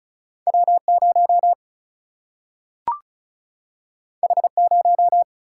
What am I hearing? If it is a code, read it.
Morse code, heard as a keyed tone, about 34 wpm, 700 Hz. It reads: W0 H0
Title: Single-Letter Single-Number 35wpm